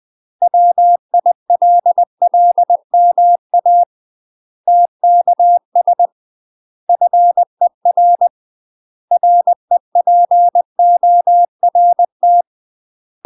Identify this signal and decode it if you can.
Morse code, heard as a keyed tone, about 20 wpm, 700 Hz. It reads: WILLMA TKS FER REPORT
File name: UK MASTER QSOs TRAINING/20WPM